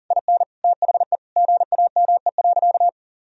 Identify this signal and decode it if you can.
Morse code, heard as a keyed tone, about 40 wpm, 700 Hz. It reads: IN THE GAME.
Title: Sentences from Top 1000 Words - Review Part 4 - 40wpm